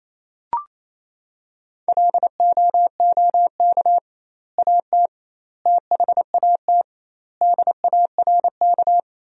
Morse code, heard as a keyed tone, about 28 wpm, 700 Hz. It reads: LOOK AT THAT DARK
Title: 2x Word-Spacing - Sentences from Top 500 Words - 28wpm